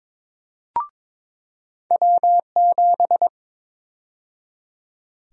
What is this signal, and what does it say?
Morse code, heard as a keyed tone, about 21 wpm, 700 Hz. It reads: W7
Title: USA Call Sign Prefixes - 22wpm